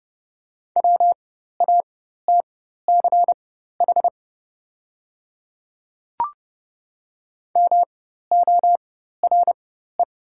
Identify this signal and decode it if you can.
Morse code, heard as a keyed tone, about 30 wpm, spaced as for 14 wpm, 700 Hz. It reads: WATCH MORE
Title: 4x Character Spacing - Top 300 Words - 30wpm